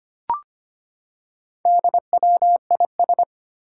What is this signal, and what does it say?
Morse code, heard as a keyed tone, about 25 wpm, 700 Hz. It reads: DWIS
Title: Four Challenging Letter-Number Combinations 25wpm